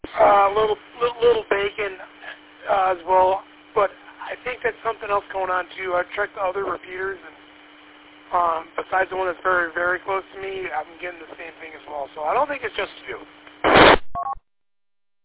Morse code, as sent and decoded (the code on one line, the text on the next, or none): none